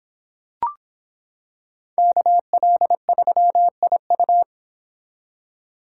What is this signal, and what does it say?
Morse code, heard as a keyed tone, about 25 wpm, 700 Hz. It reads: KL3IU